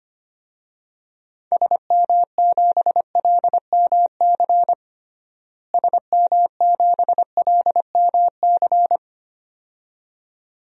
Morse code, heard as a keyed tone, about 25 wpm, 700 Hz. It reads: SM7LMC SM7LMC
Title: DXCC-100 Call Signs - 6 Characters - Repeated 2x - 25wpm